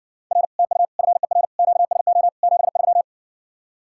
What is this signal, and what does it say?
Morse code, heard as a keyed tone, about 60 wpm, 700 Hz. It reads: R TU FER QSO 73